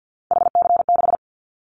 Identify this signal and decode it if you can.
Morse code, heard as a keyed tone, about 50 wpm, 700 Hz. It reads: HC6
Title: Callsign Prefixes for Top 100 DXCC - 50wpm